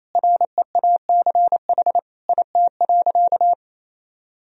Morse code, heard as a keyed tone, about 28 wpm, 700 Hz. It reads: REACH IT.
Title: Sentences from Top 400 Words and Abbreviations 28wpm